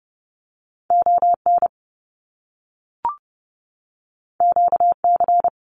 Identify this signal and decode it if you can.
Morse code, heard as a keyed tone, about 30 wpm, 700 Hz. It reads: ON QC
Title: Canadian Provinces and Territories 30wpm